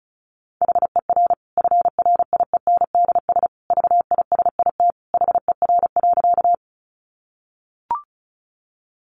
Morse code, heard as a keyed tone, about 34 wpm, 700 Hz. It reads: HER FRIENDS VISIT HER.
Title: Easy Phrases 35wpm